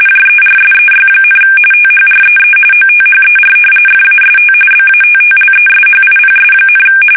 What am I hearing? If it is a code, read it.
RTTY, about 45.45 baud. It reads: JERSEY, THREE OTHER FAST BATTLESHIPS,
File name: fsk850 / battleshipnj